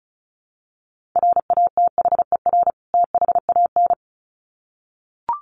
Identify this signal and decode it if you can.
Morse code, heard as a keyed tone, about 34 wpm, 700 Hz. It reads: RATHER THAN
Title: Sets of 2 Words 35wpm